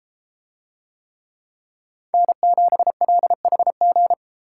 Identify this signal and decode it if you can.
Morse code, heard as a keyed tone, about 33 wpm, 700 Hz. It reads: N7LHG